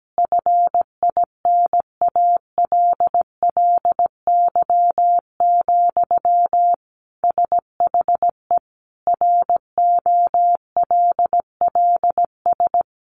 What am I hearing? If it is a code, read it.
Morse code, heard as a keyed tone, about 17 wpm, 700 Hz. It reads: FINALLY, SHE ROLLS
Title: Easy Phrases 17wpm